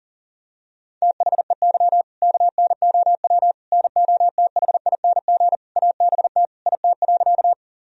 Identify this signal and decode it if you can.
Morse code, heard as a keyed tone, about 40 wpm, 700 Hz. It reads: THEY KNOW NOTHING ABT IT.
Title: Sentences from Top 500 Words and Abbreviations 40wpm